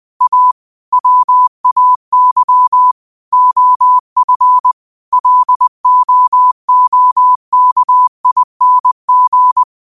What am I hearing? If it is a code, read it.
Morse code, heard as a keyed tone, about 20 wpm, 1000 Hz. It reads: A WAY OF LOOKING
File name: qotdfiles / 20wpm